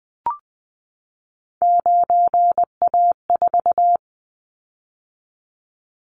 Morse code, heard as a keyed tone, about 20 wpm, 700 Hz. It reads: 9A4
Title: Callsign Prefixes for Top 50 DXCC - 20wpm